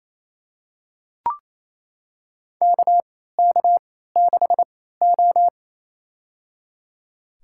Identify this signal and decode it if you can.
Morse code, heard as a keyed tone, about 28 wpm, spaced as for 16 wpm, 700 Hz. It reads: KK6O
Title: USA Call Signs - 4 Characters - 3x Character Spacing - 28wpm